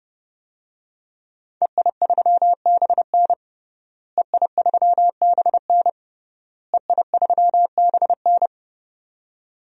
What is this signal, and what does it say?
Morse code, heard as a keyed tone, about 32 wpm, 700 Hz. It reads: EI3BN EI3BN EI3BN
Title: DXCC-100 Call Signs - 5 Characters - Repeated 3x - 30wpm